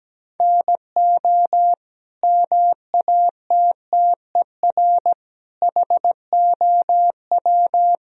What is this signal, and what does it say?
Morse code, heard as a keyed tone, about 17 wpm, 700 Hz. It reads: NO MATTER HOW
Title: Sentences from Top 1000 Words - Review Part 1 - 17wpm